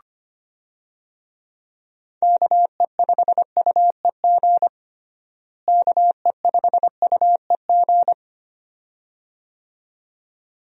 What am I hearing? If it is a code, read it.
Morse code, heard as a keyed tone, about 25 wpm, 700 Hz. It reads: KE5UEG KE5UEG
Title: USA Call Signs - 6 Characters - Repeated 2x - 25wpm